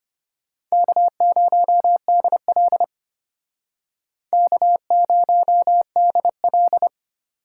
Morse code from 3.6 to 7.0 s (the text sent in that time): K0DL